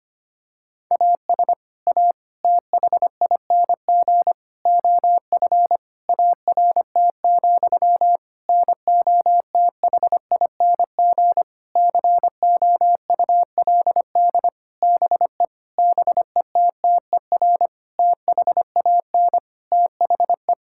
Morse code, heard as a keyed tone, about 25 wpm, 700 Hz. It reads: AS A THING OF ART, NOTHING COULD BE BETTER THAN THE
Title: Sentences from Top 1000 Words - Review Part 6 - 25wpm